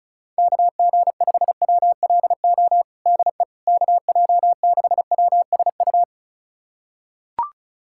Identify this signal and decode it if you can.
Morse code, heard as a keyed tone, about 34 wpm, 700 Hz. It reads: KG5WLO DE KJ6WSU
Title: USA Callsign DE Callsign 35wpm